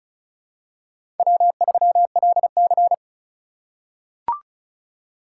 Morse code, heard as a keyed tone, about 36 wpm, 700 Hz. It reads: W3LC
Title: USA Call Signs - 4 Characters - 3x Character Spacing - 35wpm